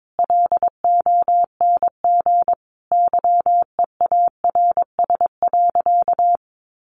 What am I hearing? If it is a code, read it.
Morse code, heard as a keyed tone, about 22 wpm, 700 Hz. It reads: LONG YEARS.
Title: Sentences from Top 200 Words 22wpm